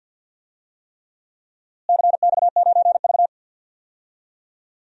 Morse code, heard as a keyed tone, about 47 wpm, 700 Hz. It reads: CX9V
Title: DXCC-100 Call Signs - 4 Characters - Repeated 3x - 50wpm